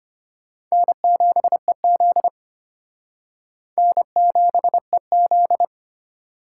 Morse code, heard as a keyed tone, about 31 wpm, 700 Hz. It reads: N7EZ N7EZ
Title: Speed Racing - USA 1x2 Call Signs 25wpm